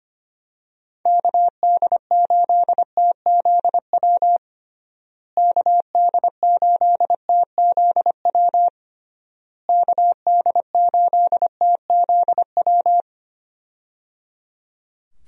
Morse code, heard as a keyed tone, about 25 wpm, 700 Hz. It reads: KD8TZW KD8TZW KD8TZW
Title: USA Call Signs - 6 Characters - Repeated 3x - 25wpm